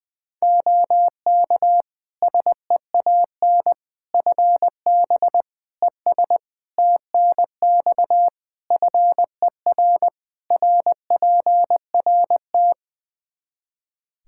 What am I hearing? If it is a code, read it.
Morse code, heard as a keyed tone, about 20 wpm, 700 Hz. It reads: OK SEAN FB ES TNX FER RPRT
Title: Practice QSO - Exchange 2 - Info - 20wpm